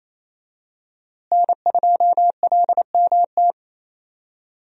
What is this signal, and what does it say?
Morse code, heard as a keyed tone, about 28 wpm, 700 Hz. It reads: N2LMT
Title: USA Call Signs - 5 Characters - Repeated 3x - 28wpm